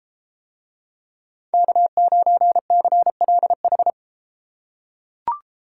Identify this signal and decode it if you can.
Morse code, heard as a keyed tone, about 33 wpm, 700 Hz. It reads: K9CLH